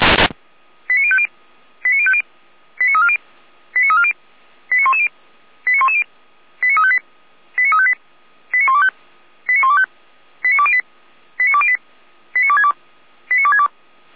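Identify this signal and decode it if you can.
Five-tone selective call (ZVEI), heard as quick runs of five tones, tones of about 70 ms. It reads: 89050 89050 89340 89340 89110 89110 89358 89358 89125 89125 89208 89208 89262 89262
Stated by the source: ZVEI five-tone selective call 89140 received by SDR.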